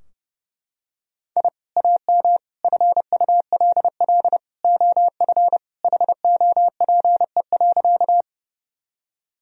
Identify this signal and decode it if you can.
Morse code, heard as a keyed tone, about 30 wpm, 700 Hz. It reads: I AM FULL OF HOPE.
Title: Sentences from Top 1000 Words - Review Part 3 - 30wpm